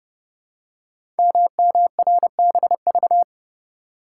Morse code, heard as a keyed tone, about 30 wpm, 700 Hz. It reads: MMRBV
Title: Code Groups - 5 Letters - 30wpm